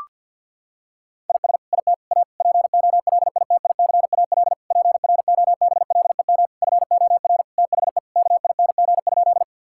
Morse code, heard as a keyed tone, about 51 wpm, 700 Hz. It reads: IS IT A POLITICAL PROBLEM FOR THE KING?
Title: Sentences from Top 1000 Words - Review Part 5 - 50wpm